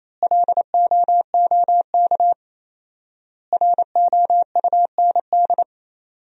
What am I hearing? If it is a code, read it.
Morse code, heard as a keyed tone, about 27 wpm, 700 Hz. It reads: LOOK ROUND
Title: 4x Word-Spacing - Sentences from Top 200 Words - 28wpm